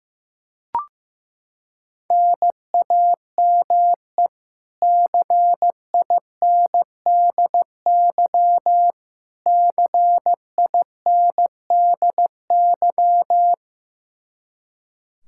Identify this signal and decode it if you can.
Morse code, heard as a keyed tone, about 16 wpm, 700 Hz. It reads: NAME CINDY CINDY
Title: K1USN SST QSO Practice 15wpm